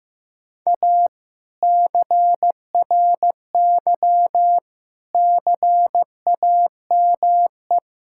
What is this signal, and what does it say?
Morse code, heard as a keyed tone, about 16 wpm, 700 Hz. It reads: A CRY CAME
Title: Sentences from Top 1000 Words - Review Part 5 - 15wpm